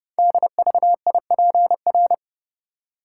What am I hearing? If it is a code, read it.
Morse code, heard as a keyed tone, about 29 wpm, 700 Hz. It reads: DVIPR
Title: Code Groups - 5 Letters - 30wpm